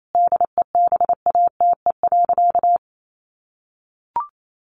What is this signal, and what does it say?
Morse code, heard as a keyed tone, about 28 wpm, 700 Hz. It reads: DEBATE.
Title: Sentences from Top 1100 Words 28wpm